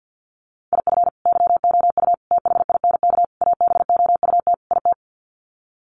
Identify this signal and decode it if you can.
Morse code, heard as a keyed tone, about 50 wpm, 700 Hz. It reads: IF YOU THINK ABOUT IT
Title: Sets of 5 Words Encore2 50wpm